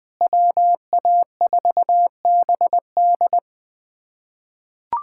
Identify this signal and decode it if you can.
Morse code, heard as a keyed tone, about 21 wpm, 700 Hz. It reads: WA4BD
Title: USA Callsign DE Callsign - 5 Characters - 20wpm